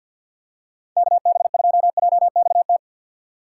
Morse code, heard as a keyed tone, about 52 wpm, 700 Hz. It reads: KB2JXT
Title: USA Call Signs - 6 Characters - 6x Character Spacing - 50wpm